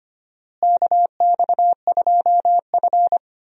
Morse code, heard as a keyed tone, about 25 wpm, 700 Hz. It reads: KX2F
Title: USA Call Signs - 4 Characters - Repeated 2x - 25wpm